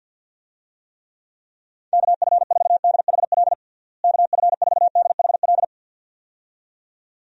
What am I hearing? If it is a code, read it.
Morse code, heard as a keyed tone, about 50 wpm, 700 Hz. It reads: KF4DHL KF4DHL
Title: USA Call Signs - 6 Characters - Repeated 2x - 50wpm